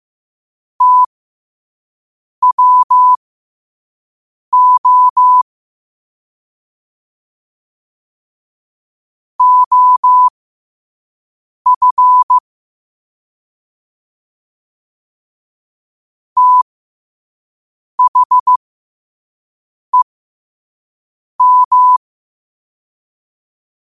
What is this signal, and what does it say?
Morse code, heard as a keyed tone, about 15 wpm, spaced as for 5 wpm, 1000 Hz. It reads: TWO OF THEM
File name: 5wpm